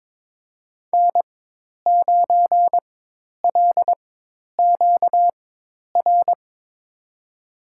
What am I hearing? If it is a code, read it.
Morse code, heard as a keyed tone, about 22 wpm, spaced as for 10 wpm, 700 Hz. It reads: N9LQR
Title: USA Call Signs - 5 Characters - 4x Character Spacing - 22wpm